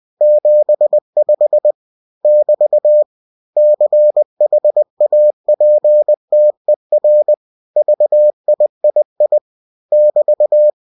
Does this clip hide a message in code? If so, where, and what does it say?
Yes: Morse code from 0.2 to 10.7 s, heard as a keyed tone, about 20 wpm, 600 Hz. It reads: 75 <BT> CHAPTER VIII <BT>